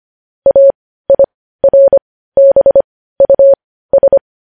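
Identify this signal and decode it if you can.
Morse code, heard as a keyed tone, about 25 wpm, spaced as for 15 wpm, 550 Hz. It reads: AIRBUS
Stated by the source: Morse code news headlines at 15.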